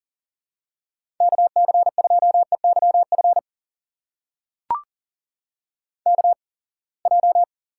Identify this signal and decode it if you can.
Morse code, heard as a keyed tone, about 38 wpm, spaced as for 27 wpm, 700 Hz. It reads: KC2EYF KJ
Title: USA Call Signs - 6 Characters - 8x Character Spacing - 40wpm